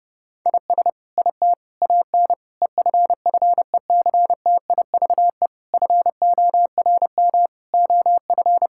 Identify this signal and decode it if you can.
Morse code, heard as a keyed tone, about 32 wpm, 700 Hz. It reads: IS IT AN EFFECTIVE FORM OF
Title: Sentences from Top 1000 Words - Review Part 6 - 30wpm